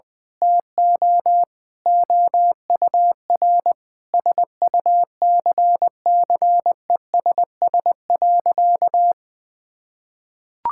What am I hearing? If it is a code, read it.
Morse code, heard as a keyed tone, about 20 wpm, 700 Hz. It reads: TO OUR SUCCESS.
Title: Sentences with 9-Word Phrases - Spoken Beginning - 20wpm